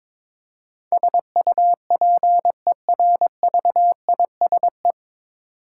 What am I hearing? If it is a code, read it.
Morse code, heard as a keyed tone, about 22 wpm, 700 Hz. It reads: SUPERVISE
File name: 9 Letter Words - First 3 Letters Given - 22wpm